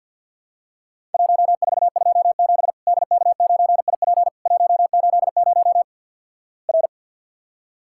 Morse code, heard as a keyed tone, about 50 wpm, 700 Hz. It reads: 1427 DK9IP 180 R
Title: Worked All Europe DX Contest - QTC Exchange - 50wpm